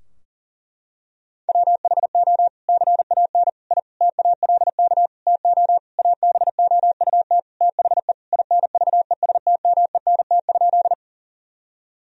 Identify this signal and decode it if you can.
Morse code, heard as a keyed tone, about 40 wpm, 700 Hz. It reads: WHO CAN I TALK TO ABOUT THE INVESTMENT?
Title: Sentences from Top 1000 Words - Review Part 4 - 40wpm